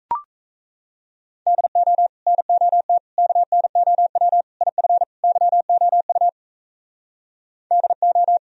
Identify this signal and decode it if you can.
Morse code, heard as a keyed tone, about 38 wpm, 700 Hz. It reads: DO NOT KNOW IF YOU DO